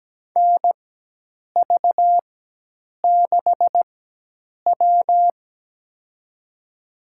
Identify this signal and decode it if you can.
Morse code, heard as a keyed tone, about 17 wpm, spaced as for 8 wpm, 700 Hz. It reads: NV6W
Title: USA Call Signs - 4 Characters - 4x Character Spacing - 17wpm